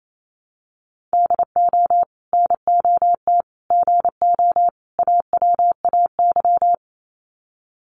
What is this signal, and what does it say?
Morse code, heard as a keyed tone, about 28 wpm, 700 Hz. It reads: DO NOT GO AWAY
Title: Sets of 4 Words 28wpm